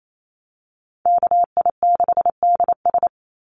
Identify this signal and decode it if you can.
Morse code, heard as a keyed tone, about 28 wpm, 700 Hz. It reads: KI6DS
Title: NA SOTA Chaser Callsigns 28wpm